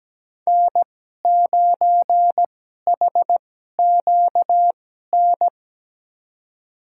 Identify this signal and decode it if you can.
Morse code, heard as a keyed tone, about 17 wpm, spaced as for 12 wpm, 700 Hz. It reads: N9HQN
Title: USA Call Signs - 2x Character Spacing - 17wpm